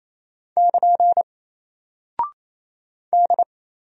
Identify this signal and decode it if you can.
Morse code, heard as a keyed tone, about 28 wpm, 700 Hz. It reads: <KN> D
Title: Single Character (Letter:Number:Punctuation:Prosign) 28wpm